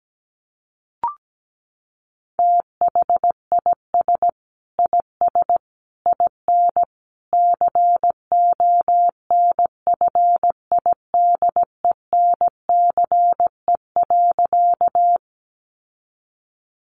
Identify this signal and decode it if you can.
Morse code, heard as a keyed tone, about 17 wpm, 700 Hz. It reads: THIS IS IN CONFIDENCE.
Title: Sentences from Top 1500 Words 17wpm